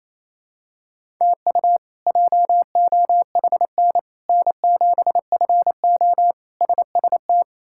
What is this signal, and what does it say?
Morse code, heard as a keyed tone, about 27 wpm, 700 Hz. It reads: TU JOHN N7FO SST
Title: K1USN SST QSO Practice - 28wpm